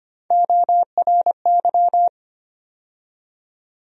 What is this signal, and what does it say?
Morse code, heard as a keyed tone, about 25 wpm, 700 Hz. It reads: ORY
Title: USA Call Sign Postfixes - 3 Letters 25wpm